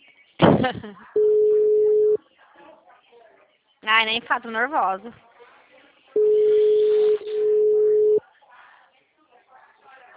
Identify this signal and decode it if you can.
Ringback tone (Europe, 425 Hz), with speech.